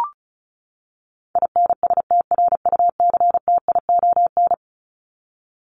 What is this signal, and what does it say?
Morse code, heard as a keyed tone, about 34 wpm, 700 Hz. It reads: INSTRUCTION
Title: Top 1751-2000 Words 35wpm